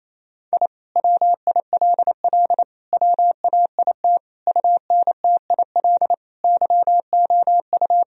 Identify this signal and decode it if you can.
Morse code, heard as a keyed tone, about 28 wpm, 700 Hz. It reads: I WILL WAIT UNTIL YOU
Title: Sentences from Top 1000 Words - Review Part 1 - 28wpm